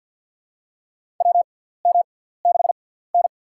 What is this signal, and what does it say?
Morse code, heard as a keyed tone, about 48 wpm, spaced as for 17 wpm, 700 Hz. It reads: WM6N
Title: USA Call Signs - 4 Characters - 6x Character Spacing - 50wpm